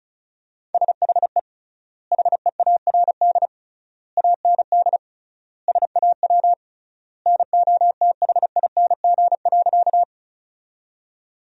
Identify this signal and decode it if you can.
Morse code, heard as a keyed tone, about 34 wpm, 700 Hz. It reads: SHE HEARD AND SAW NOTHING.
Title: 3x Word-Spacing - Sentences from Top 500 Words - 35wpm